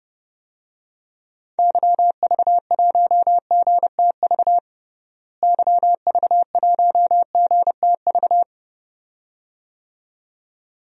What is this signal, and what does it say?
Morse code, heard as a keyed tone, about 30 wpm, 700 Hz. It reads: YV1GTV YV1GTV
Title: DXCC-100 Call Signs - 6 Characters - Repeated 2x - 30wpm